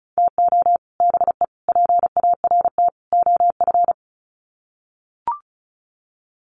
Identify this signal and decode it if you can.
Morse code, heard as a keyed tone, about 34 wpm, 700 Hz. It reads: TO BE PART OF
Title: Sets of 4 Words 35wpm